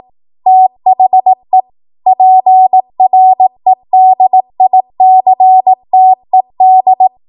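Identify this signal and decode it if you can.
Morse code, heard as a keyed tone, about 18 wpm, 750 Hz. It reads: THE PREDICTED
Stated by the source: ARRL 18 wpm Morse code practice archive transmission.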